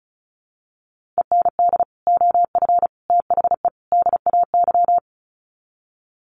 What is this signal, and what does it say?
Morse code, heard as a keyed tone, about 34 wpm, 700 Hz. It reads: END OF THE DAY
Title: Sets of 4 Words 35wpm